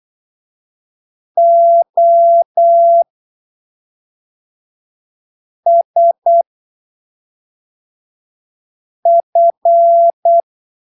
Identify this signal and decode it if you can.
Morse code, heard as a keyed tone, about 8 wpm, 680 Hz. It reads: O S F
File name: arquivos / texto / texto 10 08ppm